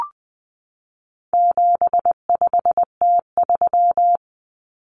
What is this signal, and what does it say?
Morse code, heard as a keyed tone, about 20 wpm, 700 Hz. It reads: 75T3
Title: Common Cut Numbers - 4-Digit 20wpm